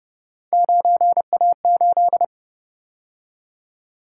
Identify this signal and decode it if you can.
Morse code, heard as a keyed tone, about 32 wpm, 700 Hz. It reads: 9A8